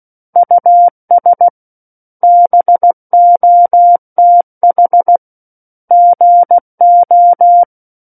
Morse code, heard as a keyed tone, about 16 wpm, 700 Hz. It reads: US BOTH GO
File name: 03-Chapter-03 16 WPM